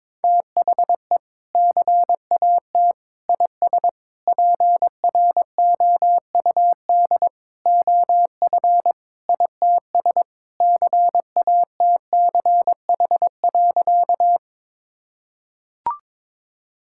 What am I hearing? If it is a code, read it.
Morse code, heard as a keyed tone, about 22 wpm, 700 Hz. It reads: THE CAT IS PROUD OF ITS CATCH.
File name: practice / 22wpm / Sentences with 2-Word Phrases - 22wpm